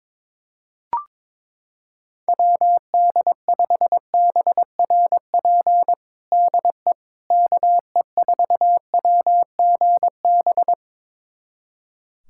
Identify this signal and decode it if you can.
Morse code, heard as a keyed tone, about 22 wpm, 700 Hz. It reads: WD5BRP DE KE4WGB
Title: USA Callsign DE Callsign 22wpm